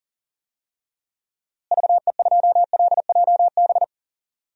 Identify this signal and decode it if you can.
Morse code, heard as a keyed tone, about 38 wpm, 700 Hz. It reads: VE2LJB
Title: DXCC-100 Call Signs - 6 Characters - Repeated 2x - 40wpm